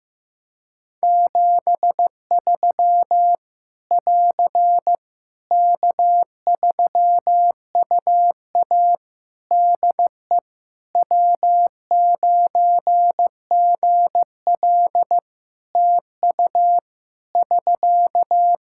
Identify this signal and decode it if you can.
Morse code, heard as a keyed tone, about 15 wpm, 700 Hz. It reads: 73 <AR> K3UA DE W9GL TU <SK>
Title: Practice QSO - Exchange 4 - Ending - 15wpm